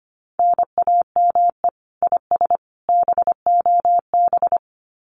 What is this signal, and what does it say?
Morse code, heard as a keyed tone, about 25 wpm, 700 Hz. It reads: NAME IS BOB
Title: Common QSO Phrases 25wpm